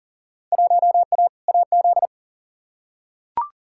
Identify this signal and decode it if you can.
Morse code, heard as a keyed tone, about 40 wpm, 700 Hz. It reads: 1A AZ
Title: Field Day Exchanges - 40wpm